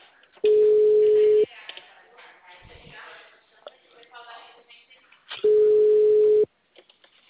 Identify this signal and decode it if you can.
Ringback tone (Europe, 425 Hz), with speech.